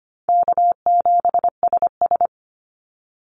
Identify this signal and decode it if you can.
Morse code, heard as a keyed tone, about 25 wpm, 700 Hz. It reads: K7SS
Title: Call Signs - 25wpm